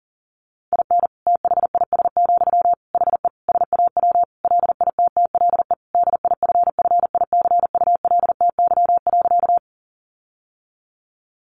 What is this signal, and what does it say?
Morse code, heard as a keyed tone, about 37 wpm, 700 Hz. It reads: IN THIS, HE SAW LITTLE DIFFICULTY.
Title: Sentences from Top 1500 Words 40wpm